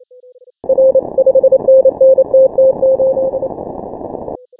Morse code, heard as a keyed tone, about 26 wpm, 530 Hz. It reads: R 5NN TT8
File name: final / W1UJ-168-140-MorseRunner